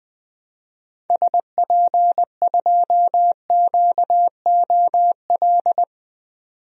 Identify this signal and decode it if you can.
Morse code, heard as a keyed tone, about 20 wpm, 700 Hz. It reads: SP2QOL
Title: DXCC-100 Call Signs - 6 Characters - 20wpm